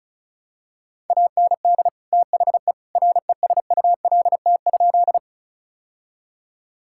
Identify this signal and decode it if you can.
Morse code, heard as a keyed tone, about 34 wpm, 700 Hz. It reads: AND THE RESULT?
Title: Sentences from Top 1000 Words - Review Part 5 - 35wpm